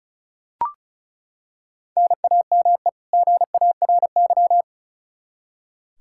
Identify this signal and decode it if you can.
Morse code, heard as a keyed tone, about 36 wpm, 700 Hz. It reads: NAME GARY
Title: Practice QSO - Exchange 1 - Beginning - 35wpm